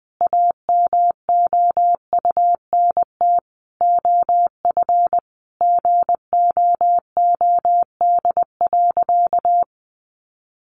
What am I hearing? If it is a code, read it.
Morse code, heard as a keyed tone, about 20 wpm, 700 Hz. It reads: AMOUNT OF GOOD.
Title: Sentences from Top 1300 Words 20wpm